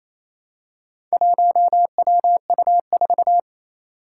Morse code, heard as a keyed tone, about 28 wpm, 700 Hz. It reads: 1WU4
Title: Four Challenging Letter-Number Combinations 28wpm